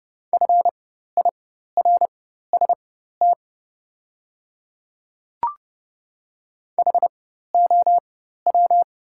Morse code, heard as a keyed tone, about 30 wpm, spaced as for 14 wpm, 700 Hz. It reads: FIRST HOW
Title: 4x Character Spacing - Top 100 Words - 30wpm